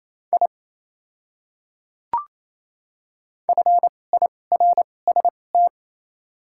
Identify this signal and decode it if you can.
Morse code, heard as a keyed tone, about 28 wpm, spaced as for 20 wpm, 700 Hz. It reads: I FIRST
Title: 2x Character Spacing - Top 100 Words - 28wpm